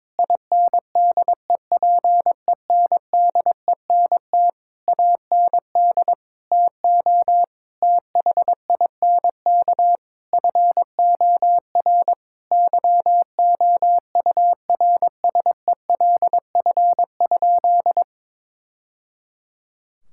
Morse code, heard as a keyed tone, about 22 wpm, 700 Hz. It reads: INDEPENDENT AND TO THINK FOR YOURSELF?
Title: Sentences from Top 1000 Words - Review Part 6 - 22wpm